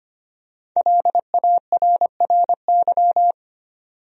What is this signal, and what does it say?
Morse code, heard as a keyed tone, about 25 wpm, 700 Hz. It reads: LARRY
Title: USA Common Names - 25wpm